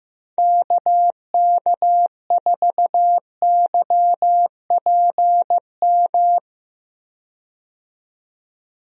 Morse code, heard as a keyed tone, about 15 wpm, 700 Hz. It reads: KK4YPM
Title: USA Call Signs - 6 Characters - 15wpm